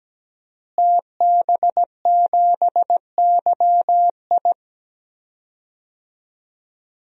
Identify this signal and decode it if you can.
Morse code, heard as a keyed tone, about 17 wpm, 700 Hz. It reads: TB7YI
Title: DXCC-100 Call Signs - 5 Characters - Repeated 2x - 17wpm